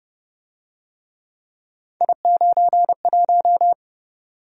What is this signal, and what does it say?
Morse code, heard as a keyed tone, about 29 wpm, 700 Hz. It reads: I91